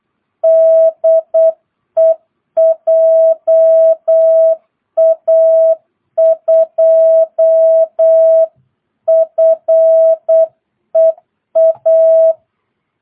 Morse code, heard as a keyed tone, about 8 wpm, 660 Hz. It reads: DE JA2FEA
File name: jh8cbh / 2022cw